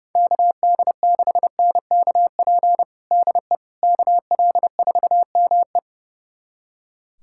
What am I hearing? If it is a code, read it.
Morse code, heard as a keyed tone, about 30 wpm, 700 Hz. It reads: KD6NKP DE KL4ME
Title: USA Callsign DE Callsign 30wpm